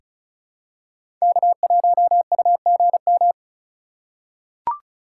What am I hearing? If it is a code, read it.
Morse code, heard as a keyed tone, about 36 wpm, 700 Hz. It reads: K1UGM
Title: USA Call Signs - 5 Characters - 4x Character Spacing - 35wpm